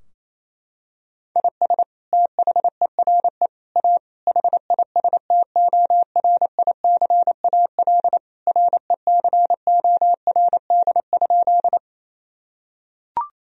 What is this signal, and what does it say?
Morse code, heard as a keyed tone, about 27 wpm, 700 Hz. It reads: IS THERE A HISTORICAL RECORD?
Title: Sentences from Top 1000 Words - Review Part 4 - 28wpm